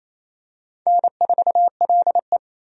Morse code, heard as a keyed tone, about 27 wpm, 700 Hz. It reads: N4LE
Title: USA Call Signs - 4 Characters - 28wpm